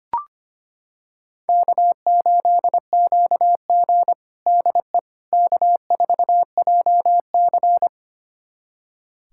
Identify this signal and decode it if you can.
Morse code, heard as a keyed tone, about 25 wpm, 700 Hz. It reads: K8QG DE K4JC
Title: USA Callsign DE Callsign - 4 Characters - 25wpm